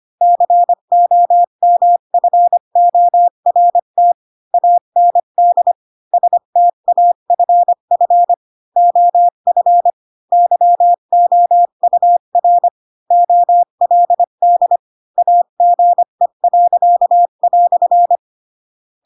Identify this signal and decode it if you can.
Morse code, heard as a keyed tone, about 25 wpm, 700 Hz. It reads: COMFORT AND STAFF OF YOUR OLD AGE."
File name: Pinocchio / Pinocchio ch8 20 25 ik1hge